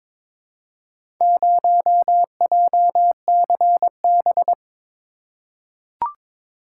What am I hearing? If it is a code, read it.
Morse code, heard as a keyed tone, about 22 wpm, 700 Hz. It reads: 0JCB
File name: practice / 22wpm / Four Letter-Number Combinations - 22wpm